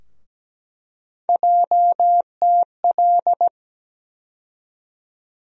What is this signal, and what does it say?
Morse code, heard as a keyed tone, about 17 wpm, 700 Hz. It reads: JTL
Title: USA Call Sign Postfixes - 3 Letters 17wpm